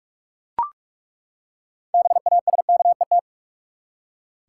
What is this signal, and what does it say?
Morse code, heard as a keyed tone, about 42 wpm, 700 Hz. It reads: BASKET
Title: Six Letter Words - 45wpm